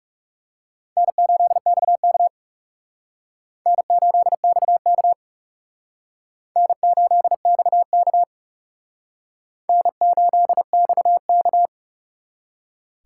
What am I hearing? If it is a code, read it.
Morse code, heard as a keyed tone, about 37 wpm, 700 Hz. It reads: N8XK N8XK N8XK N8XK
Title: Speed Racing - USA 1x2 Call Signs 30wpm